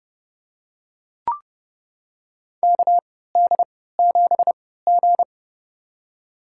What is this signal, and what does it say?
Morse code, heard as a keyed tone, about 30 wpm, spaced as for 17 wpm, 700 Hz. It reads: KD7G
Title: USA Call Signs - 4 Characters - 3x Character Spacing - 30wpm